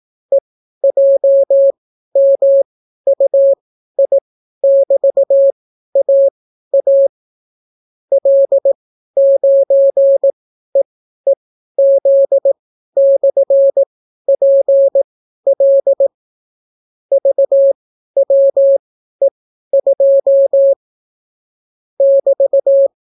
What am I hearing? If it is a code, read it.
Morse code, heard as a keyed tone, about 18 wpm, spaced as for 12 wpm, 550 Hz. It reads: EJMUI<BT>AA L9EEZ/PL VWE2 <BT>